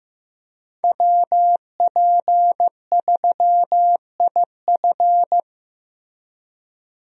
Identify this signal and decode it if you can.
Morse code, heard as a keyed tone, about 15 wpm, 700 Hz. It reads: WP3IF
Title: USA Call Signs - 5 Characters - Repeated 2x - 15wpm